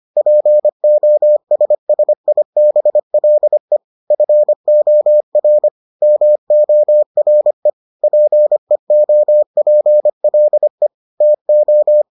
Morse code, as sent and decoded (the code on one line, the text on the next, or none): .--. --- ... ... .. -... .-.. / ..-. --- .-. / -- --- .-. / .--. --- .--. .-.. / - ---
POSSIBLE FOR MORE PEOPLE TO